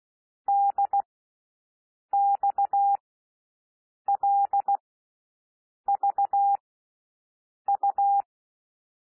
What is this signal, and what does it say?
Morse code, heard as a keyed tone, about 16 wpm, spaced as for 6 wpm, 800 Hz. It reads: DXLVU
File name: SOUND LETTERS 16 wpm